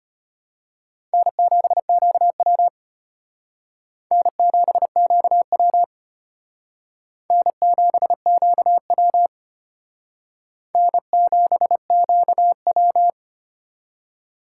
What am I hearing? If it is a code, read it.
Morse code, heard as a keyed tone, about 33 wpm, 700 Hz. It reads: N7QW N7QW N7QW N7QW